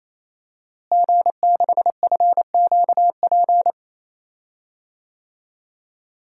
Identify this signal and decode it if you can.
Morse code, heard as a keyed tone, about 28 wpm, 700 Hz. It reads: G6FQP
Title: DXCC-100 Call Signs - 5 Characters - Repeated 3x - 28wpm